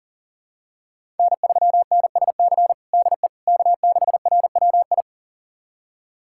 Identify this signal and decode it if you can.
Morse code, heard as a keyed tone, about 40 wpm, 700 Hz. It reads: N3NSC DE K6RWI
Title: USA Callsign DE Callsign - 5 Characters - 40wpm